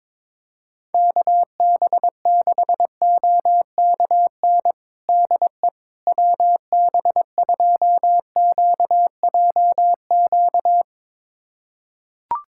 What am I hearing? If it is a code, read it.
Morse code, heard as a keyed tone, about 22 wpm, 700 Hz. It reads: KB6OKN DE WB2QJQ
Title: USA Callsign DE Callsign - 6 Characters - 22wpm